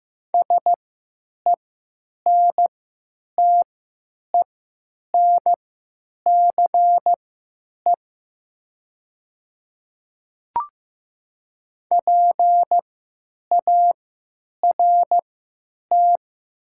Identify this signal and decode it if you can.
Morse code, heard as a keyed tone, about 15 wpm, spaced as for 9 wpm, 700 Hz. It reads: SENTENCE PART
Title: 3x Character Spacing - Top 200 Words - 15wpm